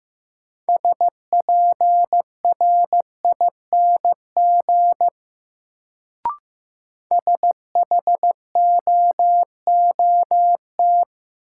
Morse code, heard as a keyed tone, about 15 wpm, 700 Hz. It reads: SPRING SHOOT
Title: Sentences from Top 1200 Words - New Word Drill Alt - 15wpm